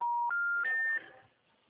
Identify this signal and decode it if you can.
Special information tone.